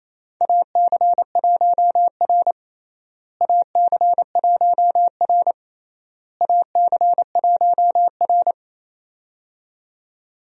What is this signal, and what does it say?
Morse code, heard as a keyed tone, about 28 wpm, 700 Hz. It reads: AC1R AC1R AC1R
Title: USA Call Signs - 4 Characters - Repeated 3x - 28wpm